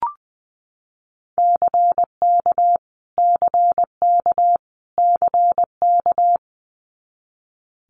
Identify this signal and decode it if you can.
Morse code, heard as a keyed tone, about 19 wpm, 700 Hz. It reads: CK CK CK
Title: Multi-letter Phonemes Repeated 3x 20wpm